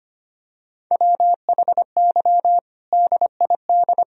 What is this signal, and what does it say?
Morse code, heard as a keyed tone, about 25 wpm, 700 Hz. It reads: WHY DID
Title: Sentences from Top 1000 Words - Review Part 3 - 25wpm